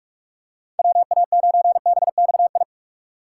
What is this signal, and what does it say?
Morse code, heard as a keyed tone, about 41 wpm, 700 Hz. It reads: WA9BXI
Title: USA Call Signs - 6 Characters - 2x Character Spacing - 45wpm